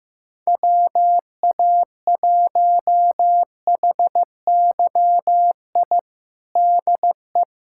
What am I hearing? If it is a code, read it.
Morse code, heard as a keyed tone, about 15 wpm, 700 Hz. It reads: WA1HYI DE
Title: USA Callsign DE Callsign - 6 Characters - 15wpm